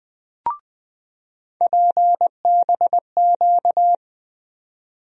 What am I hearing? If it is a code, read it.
Morse code, heard as a keyed tone, about 21 wpm, 700 Hz. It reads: PBQ
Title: USA Call Sign Postfixes - 3 Letters 20wpm